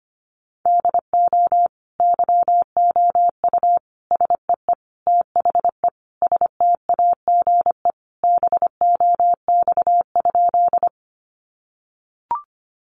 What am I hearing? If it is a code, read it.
Morse code, heard as a keyed tone, about 25 wpm, 700 Hz. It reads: DO YOU SEE THE STAGE BOX?
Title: Sentences from Top 1200 Words 25wpm